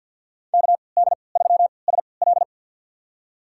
Morse code, heard as a keyed tone, about 53 wpm, spaced as for 29 wpm, 700 Hz. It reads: KD3SL
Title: USA Call Signs - 3x Character Spacing - 50wpm